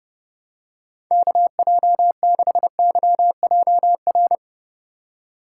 Morse code, heard as a keyed tone, about 29 wpm, 700 Hz. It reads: KJ6YJR